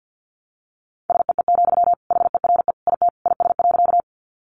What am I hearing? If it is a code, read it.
Morse code, heard as a keyed tone, about 50 wpm, 700 Hz. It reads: SEE, HERE IT IS.
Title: Sentences from Top 200 Words 50wpm